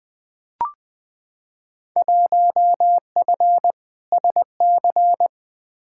Morse code, heard as a keyed tone, about 20 wpm, 700 Hz. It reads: 1F SC
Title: Field Day Exchanges Encore - 20wpm